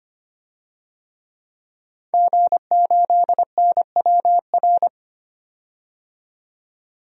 Morse code, heard as a keyed tone, about 25 wpm, 700 Hz. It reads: G8NWR
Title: DXCC-100 Call Signs - 5 Characters - 25wpm